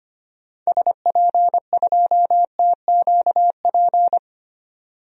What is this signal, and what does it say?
Morse code, heard as a keyed tone, about 25 wpm, 700 Hz. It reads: SP2TQP
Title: DXCC-100 Call Signs - 6 Characters - 25wpm